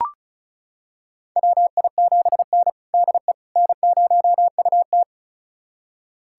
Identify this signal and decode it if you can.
Morse code, heard as a keyed tone, about 36 wpm, 700 Hz. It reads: WI7N DE N0UT
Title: USA Callsign DE Callsign - 4 Characters - 35wpm